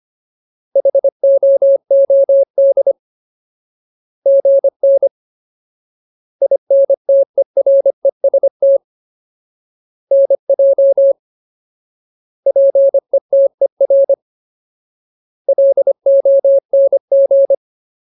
Morse code, heard as a keyed tone, about 25 wpm, 550 Hz. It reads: HOOD GN INTEREST NJ PETER LONG
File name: Ultimate Word List / 25WPM Ultimate